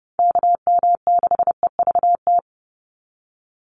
Morse code, heard as a keyed tone, about 30 wpm, 700 Hz. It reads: KM6EVT
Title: USA Call Signs 30wpm